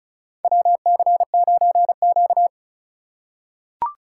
Morse code, heard as a keyed tone, about 34 wpm, 700 Hz. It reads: WC9Q
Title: USA Call Signs - 4x Character Spacing - 35wpm